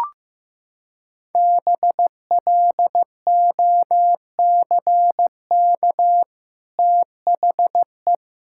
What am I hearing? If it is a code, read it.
Morse code, heard as a keyed tone, about 15 wpm, 700 Hz. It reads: BLOCK THE
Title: Sentences from Top 1600 Words 15wpm